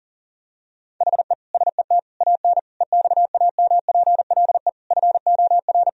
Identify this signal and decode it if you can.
Morse code, heard as a keyed tone, about 37 wpm, 700 Hz. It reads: HE SET AN EXAMPLE FOR
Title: Sentences from Top 1000 Words - Review Part 1 - 40wpm